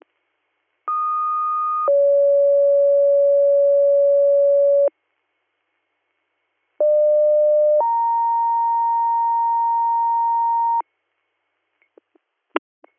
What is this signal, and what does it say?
Two-tone page, heard as a one-second tone, then a three-second tone, about 1230 then 560 Hz, about 600 then 930 Hz. It is heard from 0.9 to 10.8 s.